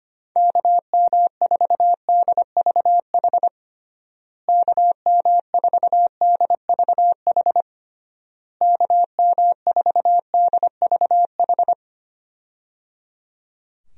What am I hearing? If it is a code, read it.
Morse code, heard as a keyed tone, about 25 wpm, 700 Hz. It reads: KM4DVH KM4DVH KM4DVH
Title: USA Call Signs - 6 Characters - Repeated 3x - 25wpm